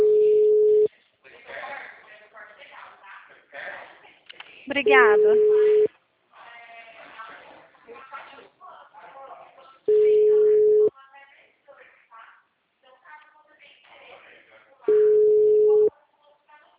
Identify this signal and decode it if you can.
Ringback tone (Europe, 425 Hz), with speech.
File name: out- -2011- - - .12386